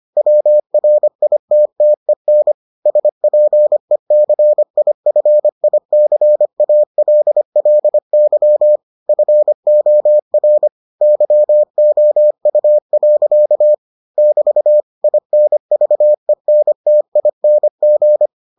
Morse code, heard as a keyed tone, about 25 wpm, 600 Hz. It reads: WRITTEN SPECIFICALLY FOR YOU. <BT> INVENTING